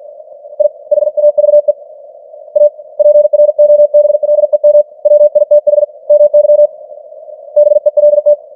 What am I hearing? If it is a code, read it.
Morse code, heard as a keyed tone, about 47 wpm, 590 Hz. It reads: I HAVE A PROBLEM WITH MY BELT
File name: archived / CWBros